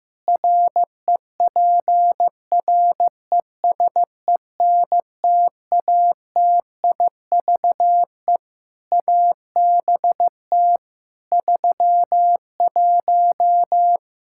Morse code, heard as a keyed tone, about 15 wpm, 700 Hz. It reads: REPRESENTATIVE ABT 31
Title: Practice QSO - Exchange 3 - More Info - 15wpm